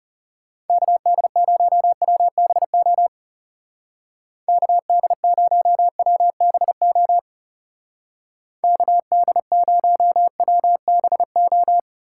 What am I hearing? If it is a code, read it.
Morse code, heard as a keyed tone, about 34 wpm, 700 Hz. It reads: KD0WBO KD0WBO KD0WBO
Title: Speed Racing - USA Call Signs 30wpm